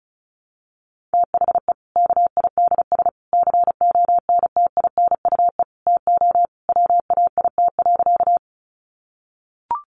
Morse code, heard as a keyed tone, about 34 wpm, 700 Hz. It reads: THE KIDS CONTINUE TO WAIT.
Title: Easy Phrases 35wpm